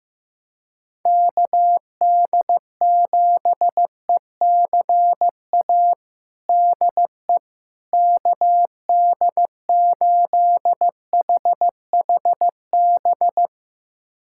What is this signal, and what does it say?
Morse code, heard as a keyed tone, about 16 wpm, 700 Hz. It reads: KD7ECA DE KD8HHB
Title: USA Callsign DE Callsign - 6 Characters - 15wpm